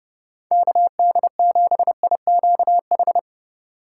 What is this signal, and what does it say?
Morse code, heard as a keyed tone, about 31 wpm, 700 Hz. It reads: KD7IQH